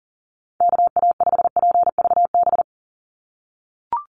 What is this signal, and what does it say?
Morse code, heard as a keyed tone, about 40 wpm, 700 Hz. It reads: KA5PVB
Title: NA SOTA Chaser Callsigns 40wpm